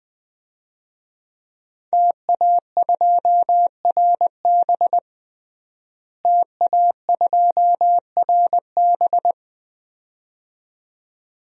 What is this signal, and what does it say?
Morse code, heard as a keyed tone, about 20 wpm, 700 Hz. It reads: TA2RB TA2RB
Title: DXCC-100 Call Signs - 5 Characters - Repeated 2x - 20wpm